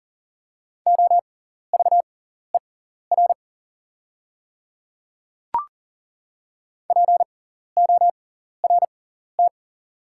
Morse code, heard as a keyed tone, about 41 wpm, spaced as for 14 wpm, 690 Hz. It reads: OVER PORT
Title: 6x Character Spacing - Top 300 Words - 40wpm